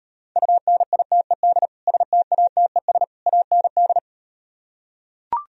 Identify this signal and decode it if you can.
Morse code, heard as a keyed tone, about 38 wpm, 700 Hz. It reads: UNITED STATES AND